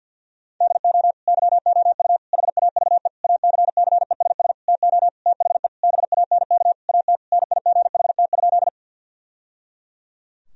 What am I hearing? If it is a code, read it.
Morse code, heard as a keyed tone, about 50 wpm, 700 Hz. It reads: DO YOU HAVE ACCESS TO THE BANK AT NIGHT?
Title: Sentences from Top 1000 Words - Review Part 4 - 50wpm